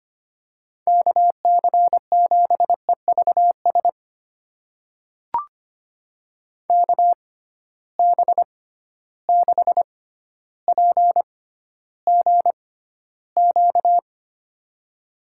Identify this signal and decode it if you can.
Morse code, heard as a keyed tone, about 25 wpm, spaced as for 13 wpm, 700 Hz. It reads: KC7EVS KB6PGQ
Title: USA Call Signs - 6 Characters - 6x Character Spacing - 25wpm